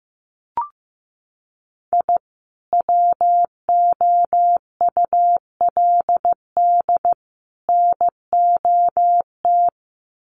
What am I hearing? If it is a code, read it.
Morse code, heard as a keyed tone, about 15 wpm, 700 Hz. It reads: I WOULD NOT
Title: Sets of 3 Words 15wpm